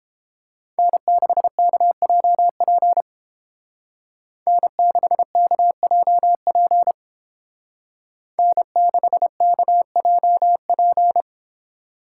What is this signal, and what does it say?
Morse code, heard as a keyed tone, about 31 wpm, 700 Hz. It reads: N6KJP N6KJP N6KJP